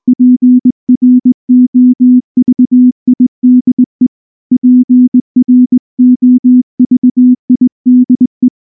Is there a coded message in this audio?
Yes: Morse code from 0.1 to 8.5 s, heard as a keyed tone, about 20 wpm, 250 Hz. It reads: PROVIDE PROVIDE